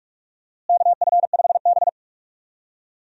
Morse code, heard as a keyed tone, about 46 wpm, 700 Hz. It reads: KF5B
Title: USA Call Signs - 4 Characters - Repeated 2x - 45wpm